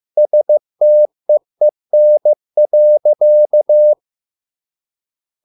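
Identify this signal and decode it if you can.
Morse code, heard as a keyed tone, about 15 wpm, 600 Hz. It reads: STEEN.
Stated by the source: Morse code at a slow 15 wpm from CW med Gnister.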